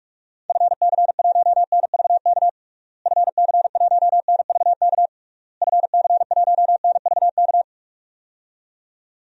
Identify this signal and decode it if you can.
Morse code, heard as a keyed tone, about 46 wpm, 700 Hz. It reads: FC1NVK FC1NVK FC1NVK
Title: DXCC-100 Call Signs - 6 Characters - Repeated 3x - 45wpm